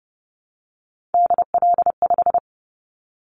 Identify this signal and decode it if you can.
Morse code, heard as a keyed tone, about 30 wpm, 700 Hz. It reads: DL5
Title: Callsign Prefixes for Top 50 DXCC - 30wpm